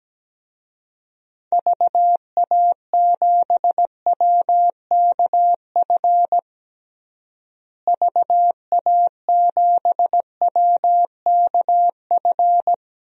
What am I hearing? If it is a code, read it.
Morse code, heard as a keyed tone, about 17 wpm, 700 Hz. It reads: VA7WKF VA7WKF
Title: DXCC-100 Call Signs - 6 Characters - Repeated 3x - 17wpm